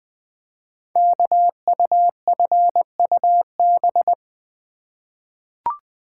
Morse code, heard as a keyed tone, about 20 wpm, 700 Hz. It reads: KUFUB
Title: Code Groups - 5 Letters - 20wpm